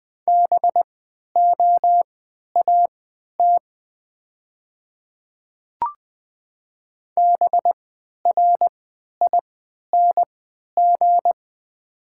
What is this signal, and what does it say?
Morse code, heard as a keyed tone, about 20 wpm, spaced as for 11 wpm, 700 Hz. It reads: BOAT BRING
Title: 3x Character Spacing - Top 500 Words - 20wpm